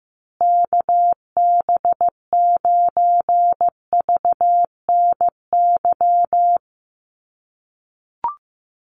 Morse code, heard as a keyed tone, about 15 wpm, 700 Hz. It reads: KB9VNY
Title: USA Call Signs Repeated 3x 15wpm